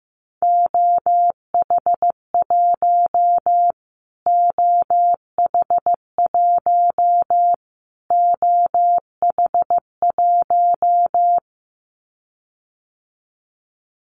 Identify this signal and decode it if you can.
Morse code, heard as a keyed tone, about 15 wpm, 700 Hz. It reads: OH1 OH1 OH1
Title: Callsign Prefixes for Top 25 DXCC - Repeated 3x - 15wpm